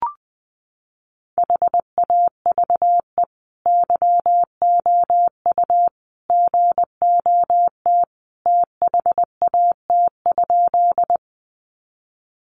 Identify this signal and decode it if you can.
Morse code, heard as a keyed tone, about 20 wpm, 700 Hz. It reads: HAVE YOU GOT THAT?
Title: Sentences from Top 300 Words 20wpm